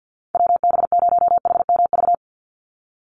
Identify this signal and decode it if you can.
Morse code, heard as a keyed tone, about 50 wpm, 700 Hz. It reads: WB0HMV
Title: USA Call Signs 50wpm